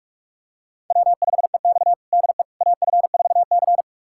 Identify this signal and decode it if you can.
Morse code, heard as a keyed tone, about 46 wpm, 700 Hz. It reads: W5EX DE AF4C
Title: USA Callsign DE Callsign - 4 Characters - 45wpm